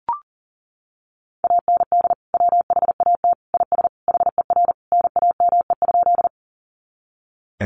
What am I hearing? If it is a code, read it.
Morse code, heard as a keyed tone, about 40 wpm, 700 Hz. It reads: AND WHAT IS HER NAME?
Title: Sentences from Top 200 Words 40wpm